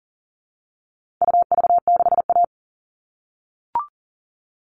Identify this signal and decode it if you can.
Morse code, heard as a keyed tone, about 40 wpm, 700 Hz. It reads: UV6A
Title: All Cut Numbers - 4-Digit 40wpm